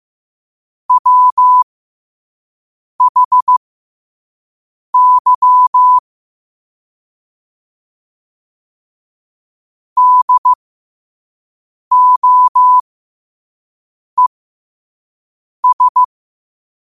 Morse code, heard as a keyed tone, about 15 wpm, spaced as for 5 wpm, 1000 Hz. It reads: WHY DOES
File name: qotdfiles / 5wpm